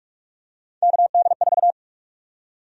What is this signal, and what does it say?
Morse code, heard as a keyed tone, about 46 wpm, 700 Hz. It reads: KD4
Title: USA Call Sign Prefixes - 45wpm